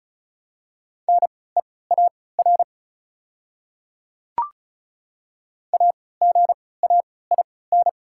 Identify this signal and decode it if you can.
Morse code, heard as a keyed tone, about 34 wpm, spaced as for 20 wpm, 700 Hz. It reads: NEAR AGAIN
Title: 3x Character Spacing - Top 300 Words - 35wpm